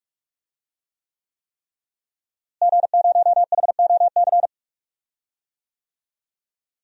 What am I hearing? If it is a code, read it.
Morse code, heard as a keyed tone, about 45 wpm, 700 Hz. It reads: G0HOC
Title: DXCC-100 Call Signs - 5 Characters - Repeated 3x - 45wpm